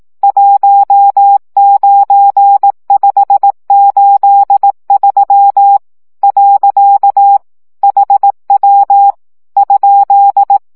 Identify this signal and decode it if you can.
Morse code, heard as a keyed tone, about 18 wpm, 800 Hz. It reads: 19583 . HW ?
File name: morze vizsga